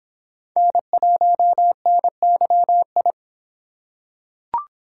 Morse code, heard as a keyed tone, about 25 wpm, 700 Hz. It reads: N1NYI